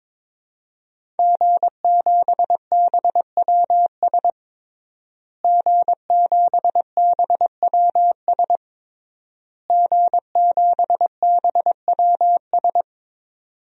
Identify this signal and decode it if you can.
Morse code, heard as a keyed tone, about 21 wpm, 700 Hz. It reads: G7BWS G7BWS G7BWS
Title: DXCC-100 Call Signs - 5 Characters - Repeated 3x - 22wpm